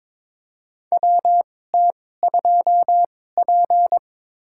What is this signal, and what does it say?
Morse code, heard as a keyed tone, about 23 wpm, spaced as for 16 wpm, 700 Hz. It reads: WT2P
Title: USA Call Signs - 4 Characters - 2x Character Spacing - 22wpm